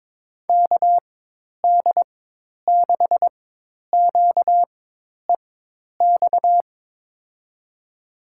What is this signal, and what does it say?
Morse code, heard as a keyed tone, about 22 wpm, spaced as for 10 wpm, 700 Hz. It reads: KD6QEX
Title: USA Call Signs - 6 Characters - 4x Character Spacing - 22wpm